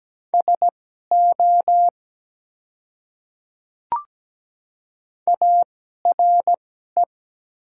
Morse code, heard as a keyed tone, about 17 wpm, spaced as for 12 wpm, 700 Hz. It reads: SO ARE
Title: 2x Character Spacing - Top 300 Words - 17wpm